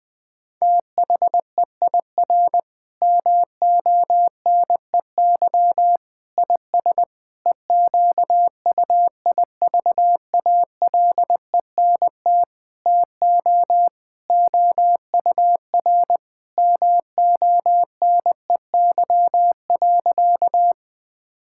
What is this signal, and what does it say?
Morse code, heard as a keyed tone, about 20 wpm, 700 Hz. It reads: THEIR MONEY IS EQUIVALENT TO OUR MONEY.
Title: Sentences from Top 1600 Words 20wpm